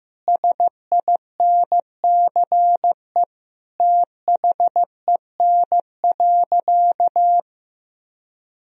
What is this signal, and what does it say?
Morse code, heard as a keyed tone, about 15 wpm, 700 Hz. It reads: SINCE THEN.
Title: Sentences from Top 1000 Words - Review Part 1 - 15wpm